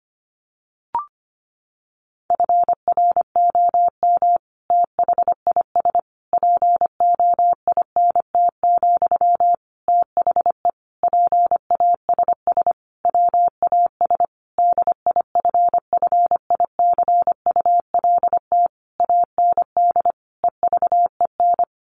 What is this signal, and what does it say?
Morse code, heard as a keyed tone, about 25 wpm, 700 Hz. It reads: FROM THIS POINT, THE PASS WAS DIFFICULT AND EVEN
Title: Sentences from Top 1200 Words 25wpm